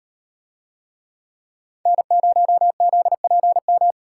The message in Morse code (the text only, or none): N0ZPM